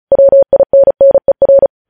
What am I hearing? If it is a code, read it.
Morse code, heard as a keyed tone, about 36 wpm, 550 Hz. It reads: WINNER
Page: news headlines in morse code at 35 wpm